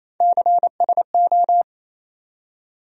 Morse code, heard as a keyed tone, about 28 wpm, 700 Hz. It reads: CSO
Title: USA Call Sign Postfixes - 3 Letters 28wpm